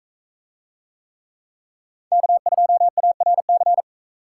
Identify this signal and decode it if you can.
Morse code, heard as a keyed tone, about 40 wpm, 700 Hz. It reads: K2ARC